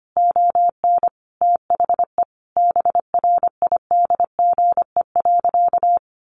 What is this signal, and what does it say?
Morse code, heard as a keyed tone, about 25 wpm, 700 Hz. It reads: ON THE BRIDGE.
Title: Sentences from Top 1200 Words 25wpm